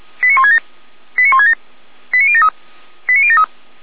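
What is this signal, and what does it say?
Five-tone selective call (ZVEI), heard as quick runs of five tones, tones of about 70 ms. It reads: 89167 89167 89073 89073
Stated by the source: ZVEI five-tone selective call received on an SDR, archived by feuerwehr-orlamuende.org.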